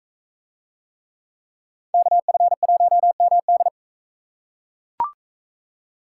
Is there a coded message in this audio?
Yes: Morse code from 1.9 to 3.7 s, heard as a keyed tone, about 41 wpm, 700 Hz. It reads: KF1MD